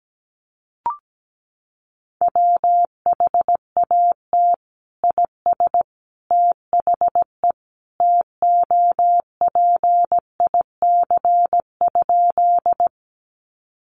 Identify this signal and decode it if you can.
Morse code, heard as a keyed tone, about 17 wpm, 700 Hz. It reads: WHAT IS THE TOPIC?
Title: Sentences from Top 1200 Words 17wpm